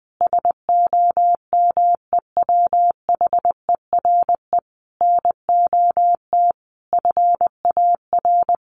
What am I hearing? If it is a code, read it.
Morse code, heard as a keyed tone, about 20 wpm, 700 Hz. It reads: SOMEWHERE NOT FAR
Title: Sentences from Top 1400 Words 20wpm